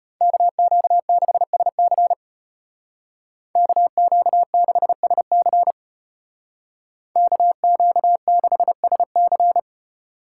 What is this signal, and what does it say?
Morse code, heard as a keyed tone, about 32 wpm, 700 Hz. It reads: KQ6SC KQ6SC KQ6SC